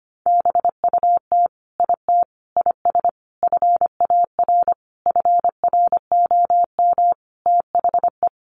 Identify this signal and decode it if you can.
Morse code, heard as a keyed tone, about 25 wpm, 700 Hz. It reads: BUT IT IS FAR FROM THE
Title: Sentences from Top 1400 Words 25wpm